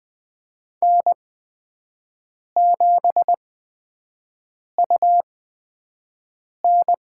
Morse code, heard as a keyed tone, about 20 wpm, spaced as for 5 wpm, 700 Hz. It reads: N7UN
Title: USA Call Signs - 4 Characters - 8x Character Spacing - 20wpm